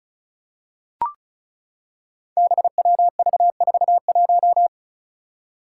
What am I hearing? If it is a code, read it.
Morse code, heard as a keyed tone, about 34 wpm, 700 Hz. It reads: BWV41
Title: Five Challenging Letter-Number Combinations 35wpm